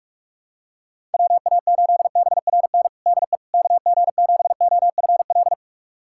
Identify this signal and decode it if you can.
Morse code, heard as a keyed tone, about 41 wpm, 700 Hz. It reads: WA8BRN DE KG7OFL
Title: USA Callsign DE Callsign - 6 Characters - 45wpm